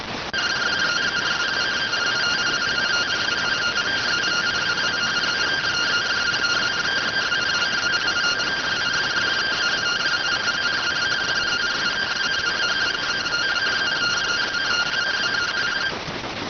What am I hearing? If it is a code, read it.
Two-tone teleprinter signal, about 170 Hz shift.